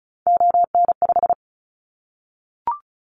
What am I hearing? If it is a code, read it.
Morse code, heard as a keyed tone, about 36 wpm, 700 Hz. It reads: ON5
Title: Callsign Prefixes for Top 100 DXCC - 35wpm